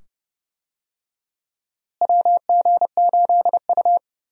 Morse code, heard as a keyed tone, about 30 wpm, 700 Hz. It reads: WG8U
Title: USA Call Signs - 4 Characters - Repeated 2x - 30wpm